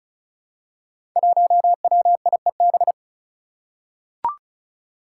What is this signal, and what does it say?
Morse code, heard as a keyed tone, about 36 wpm, 700 Hz. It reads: 1WIEB
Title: Five Challenging Letter-Number Combinations 35wpm